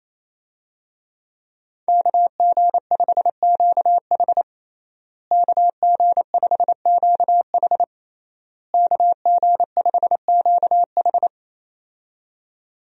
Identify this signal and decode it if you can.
Morse code, heard as a keyed tone, about 28 wpm, 700 Hz. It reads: KG5QH KG5QH KG5QH
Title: USA Call Signs - 5 Characters - Repeated 3x - 28wpm